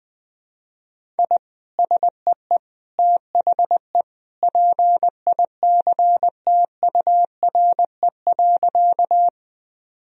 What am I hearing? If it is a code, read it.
Morse code, heard as a keyed tone, about 20 wpm, 700 Hz. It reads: I SEE THE PICTURE.
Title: Sentences from Top 1000 Words - Review Part 1 - 20wpm